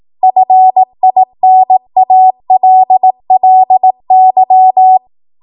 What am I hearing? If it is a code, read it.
Morse code, heard as a keyed tone, about 18 wpm, 750 Hz. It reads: FINALLY
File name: Morse / 18WPM